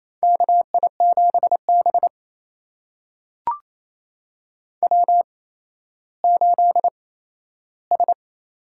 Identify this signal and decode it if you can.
Morse code, heard as a keyed tone, about 28 wpm, spaced as for 14 wpm, 700 Hz. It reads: KI7B W8S
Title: USA Call Signs - 4 Characters - 8x Character Spacing - 28wpm